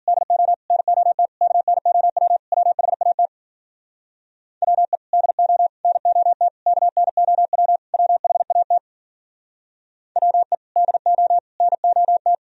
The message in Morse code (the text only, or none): DO NOT KNOW WHAT WE DO NOT KNOW WHAT WE DO NOT